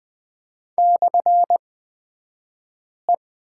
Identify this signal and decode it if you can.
Morse code, heard as a keyed tone, about 20 wpm, spaced as for 5 wpm, 700 Hz. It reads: /E
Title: 254 Lesson 41d - Period - Two Characters 20wpm